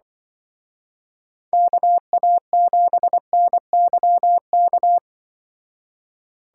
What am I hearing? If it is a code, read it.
Morse code, heard as a keyed tone, about 23 wpm, 700 Hz. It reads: KA7NYK
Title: Speed Racing - USA 2x3 Call Signs 20wpm